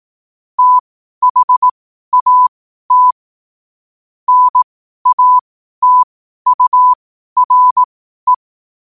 Morse code, heard as a keyed tone, about 18 wpm, spaced as for 12 wpm, 1000 Hz. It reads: THAT NATURE
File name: qotdfiles / 12wpm